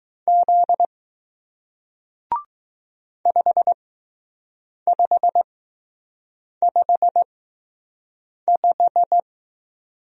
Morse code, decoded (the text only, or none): Z 5 5 5 5